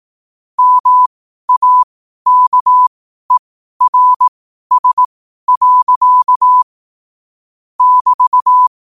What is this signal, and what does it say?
Morse code, heard as a keyed tone, about 18 wpm, spaced as for 12 wpm, 1000 Hz. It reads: MAKERS. <BT>
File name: qotdfiles / 12wpm